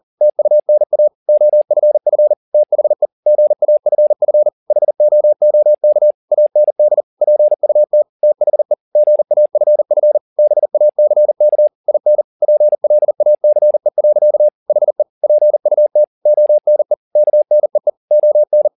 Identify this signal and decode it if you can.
Morse code, heard as a keyed tone, about 40 wpm, 600 Hz. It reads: TUNA OFF THE GAFF HOOK AND PUT THE GAFF BACK IN PLACE. HE PUT ONE KNEE ON